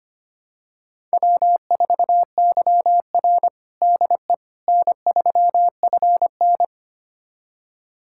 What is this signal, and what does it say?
Morse code, heard as a keyed tone, about 25 wpm, 700 Hz. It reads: W4YR DE N3FN
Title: USA Callsign DE Callsign - 4 Characters - 25wpm